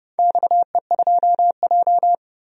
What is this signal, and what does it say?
Morse code, heard as a keyed tone, about 30 wpm, 700 Hz. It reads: XE2J
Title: DXCC-100 Call Signs - 4 Characters - 30wpm